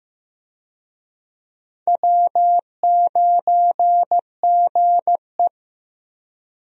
Morse code, heard as a keyed tone, about 16 wpm, 700 Hz. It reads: W9GE
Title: USA Call Signs - 4 Characters - Repeated 2x - 15wpm